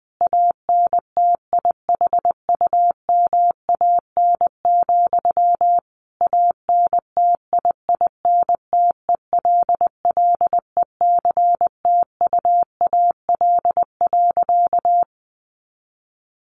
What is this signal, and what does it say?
Morse code, heard as a keyed tone, about 20 wpm, 700 Hz. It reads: ANTIHUMAN, ANTIINTELLECTUAL.
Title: A Sound Of Thunder 20wpm